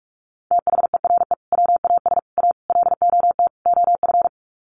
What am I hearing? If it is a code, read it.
Morse code, heard as a keyed tone, about 44 wpm, 700 Hz. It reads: THERE WAS A LOT OF
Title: Sets of 5 Words 45wpm